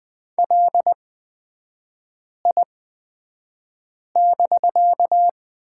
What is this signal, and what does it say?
Morse code, heard as a keyed tone, about 20 wpm, spaced as for 5 wpm, 700 Hz. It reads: LI<BK>
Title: 250 Lesson 40e - BK - Three Characters 20wpm